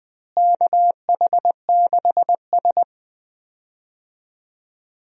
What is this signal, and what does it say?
Morse code, heard as a keyed tone, about 19 wpm, 700 Hz. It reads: KH6S
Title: DXCC-100 Call Signs - 4 Characters - Repeated 3x - 20wpm